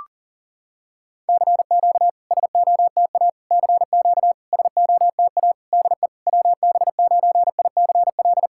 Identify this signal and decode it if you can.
Morse code, heard as a keyed tone, about 41 wpm, 690 Hz. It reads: CQ SOTA CQ SOTA DE WB9ICL
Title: SOTA QSO Practice 40wpm